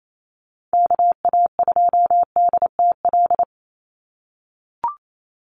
Morse code, heard as a keyed tone, about 28 wpm, 700 Hz. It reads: KA2DTL
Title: USA Call Signs 28wpm